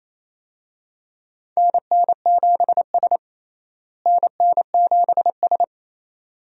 Morse code, heard as a keyed tone, about 28 wpm, 700 Hz. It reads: NN7S NN7S
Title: USA Call Signs - 4 Characters - Repeated 2x - 28wpm